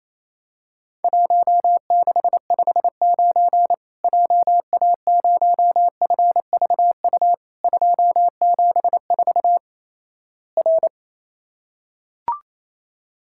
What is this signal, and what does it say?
Morse code, heard as a keyed tone, about 28 wpm, 700 Hz. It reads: 1659 JA0FVU 274 R
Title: Worked All Europe DX Contest - QTC Exchange - 28wpm